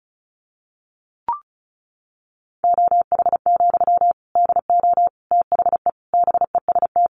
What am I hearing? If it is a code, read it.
Morse code, heard as a keyed tone, about 34 wpm, 700 Hz. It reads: OH, DO THE BEST
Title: Sentences from Top 500 Words 35wpm